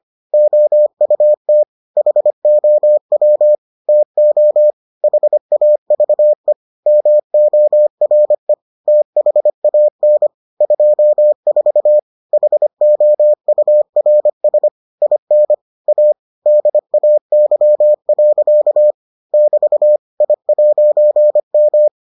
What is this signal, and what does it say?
Morse code, heard as a keyed tone, about 25 wpm, 600 Hz. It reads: OUT HOW TO HAVE MORE THAN 24 HOURS IN A DAY. <BT> I'M